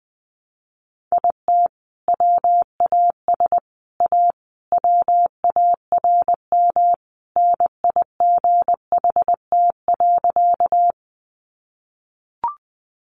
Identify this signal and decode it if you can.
Morse code, heard as a keyed tone, about 19 wpm, 700 Hz. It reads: IT WAS A WARM NIGHT.
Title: Sentences from Top 500 Words 20wpm